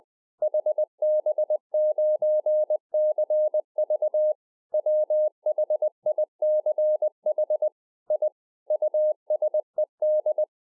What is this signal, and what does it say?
Morse code, heard as a keyed tone, about 20 wpm, 610 Hz. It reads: HB9CV WHICH I USED
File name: 2011-04-mp3 / KAZU JK1TCV 20WPM